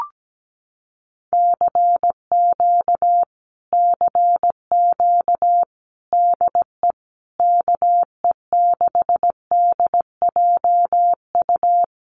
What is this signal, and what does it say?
Morse code, heard as a keyed tone, about 17 wpm, 700 Hz. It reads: CQ CQ DE KE6DJU
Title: Common QSO Phrases Encore 17wpm